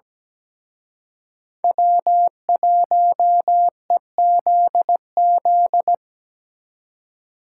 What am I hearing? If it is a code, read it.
Morse code, heard as a keyed tone, about 20 wpm, 700 Hz. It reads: W1EZZ
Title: Speed Racing - USA 1x3 Call Signs 17wpm